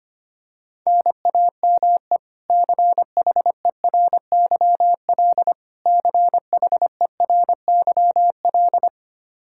Morse code, heard as a keyed tone, about 25 wpm, 700 Hz. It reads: NAME CHERYL CHERYL
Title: K1USN SST QSO Practice 25wpm